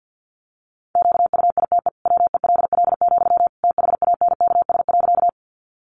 Morse code, heard as a keyed tone, about 49 wpm, 700 Hz. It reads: QUITE WELL, THANKS.
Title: Sentences from Top 700 Words 50wpm